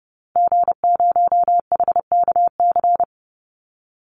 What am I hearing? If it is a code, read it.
Morse code, heard as a keyed tone, about 30 wpm, 700 Hz. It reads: G0HKC
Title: Call Signs Repeated 2x - 30wpm